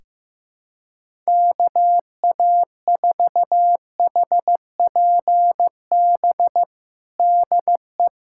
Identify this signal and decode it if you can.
Morse code, heard as a keyed tone, about 15 wpm, 700 Hz. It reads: KA4HPB DE
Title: USA Callsign DE Callsign 15wpm